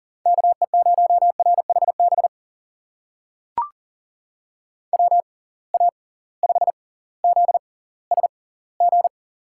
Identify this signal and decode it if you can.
Morse code, heard as a keyed tone, about 40 wpm, spaced as for 20 wpm, 700 Hz. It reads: KE0RHB WA5ZSG
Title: USA Call Signs - 6 Characters - 6x Character Spacing - 40wpm